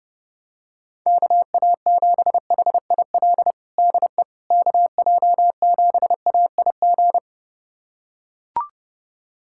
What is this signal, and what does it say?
Morse code, heard as a keyed tone, about 30 wpm, 700 Hz. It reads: KA7HIL DE KJ7AIG
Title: USA Callsign DE Callsign - 6 Characters - 30wpm